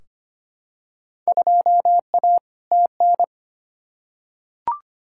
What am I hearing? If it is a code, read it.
Morse code, heard as a keyed tone, about 25 wpm, 700 Hz. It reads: 2A TN
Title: Field Day Exchanges - 25wpm